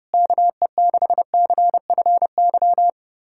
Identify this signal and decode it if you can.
Morse code, heard as a keyed tone, about 30 wpm, 700 Hz. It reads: KE6CFY
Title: USA Callsign DE Callsign 30wpm